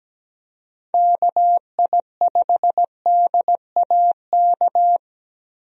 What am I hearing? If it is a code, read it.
Morse code, heard as a keyed tone, about 17 wpm, 700 Hz. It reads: KI5DAK
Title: USA Call Signs - 6 Characters - Repeated 3x - 17wpm